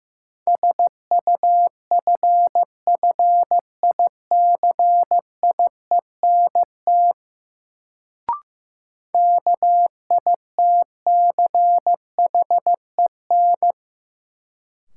Morse code, heard as a keyed tone, about 15 wpm, 700 Hz. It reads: SUFFICIENT KITCHEN
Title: Sentences from Top 1400 Words - New Word Drill - 15wpm